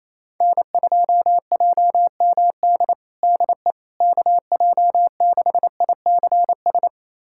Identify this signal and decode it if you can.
Morse code, heard as a keyed tone, about 28 wpm, 700 Hz. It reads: N2JMD DE KJ6ICS
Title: USA Callsign DE Callsign 28wpm